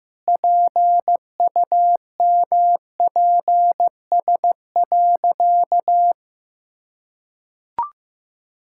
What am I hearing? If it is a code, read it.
Morse code, heard as a keyed tone, about 14 wpm, 700 Hz. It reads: PUMPS.
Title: Pangram Challenge v2 - 15wpm